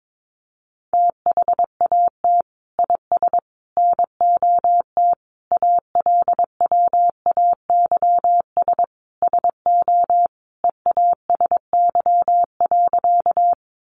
Morse code, heard as a keyed tone, about 22 wpm, 700 Hz. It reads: THAT IS NOT ALWAYS SO EASY.
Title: Sentences from Top 700 Words 22wpm